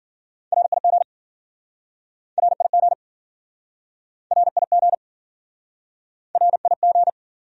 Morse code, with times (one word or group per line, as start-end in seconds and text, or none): 0.5-1.0 RIG
2.4-2.9 RIG
4.3-4.9 RIG
6.4-7.1 RIG